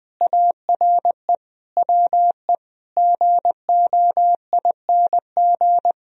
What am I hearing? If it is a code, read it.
Morse code, heard as a keyed tone, about 20 wpm, 700 Hz. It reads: ARE WE GOING